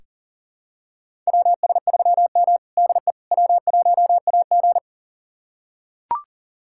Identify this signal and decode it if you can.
Morse code, heard as a keyed tone, about 40 wpm, 700 Hz. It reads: WS3M DE W1AG
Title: USA Callsign DE Callsign - 4 Characters - 40wpm